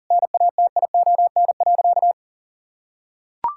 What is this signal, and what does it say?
Morse code, heard as a keyed tone, about 40 wpm, 700 Hz. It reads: NATION.
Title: Sentences from Top 1000 Words - Review Part 6 - 40wpm